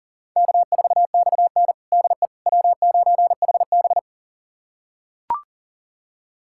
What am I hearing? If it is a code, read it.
Morse code, heard as a keyed tone, about 40 wpm, 700 Hz. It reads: K4XN DE W9HB
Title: USA Callsign DE Callsign - 4 Characters - 40wpm